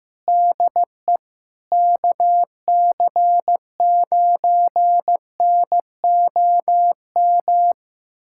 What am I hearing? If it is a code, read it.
Morse code, heard as a keyed tone, about 15 wpm, 700 Hz. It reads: DE KC9NOM
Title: USA Callsign DE Callsign - 6 Characters - 15wpm